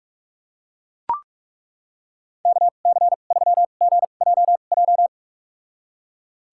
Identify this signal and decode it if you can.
Morse code, heard as a keyed tone, about 42 wpm, spaced as for 31 wpm, 700 Hz. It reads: KC3GJJ
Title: USA Call Signs - 6 Characters - 2x Character Spacing - 45wpm